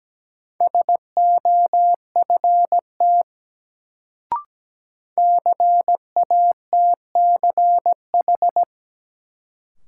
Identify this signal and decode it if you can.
Morse code, heard as a keyed tone, about 17 wpm, 700 Hz. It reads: SOFT CATCH
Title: Sentences from Top 1500 Words - New Word Drill Alt - 17wpm